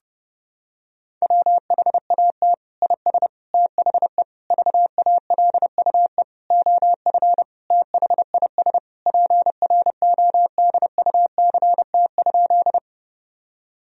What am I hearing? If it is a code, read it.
Morse code, heard as a keyed tone, about 32 wpm, 700 Hz. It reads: WHAT IS THE VALUE OF THIS PRODUCT?
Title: Sentences from Top 1000 Words - Review Part 3 - 30wpm